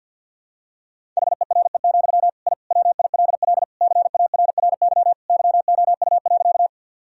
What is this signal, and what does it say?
Morse code, heard as a keyed tone, about 50 wpm, 700 Hz. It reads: HERE, I WILL CARRY YOU.
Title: Sentences from Top 1000 Words - Review Part 1 - 50wpm